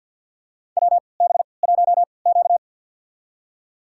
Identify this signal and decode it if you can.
Morse code, heard as a keyed tone, about 51 wpm, spaced as for 28 wpm, 700 Hz. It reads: WB1Q
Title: USA Call Signs - 3x Character Spacing - 50wpm